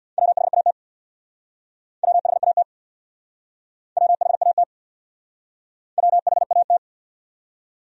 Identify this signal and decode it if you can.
Morse code, heard as a keyed tone, about 68 wpm, 700 Hz. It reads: WHAT WHAT WHAT WHAT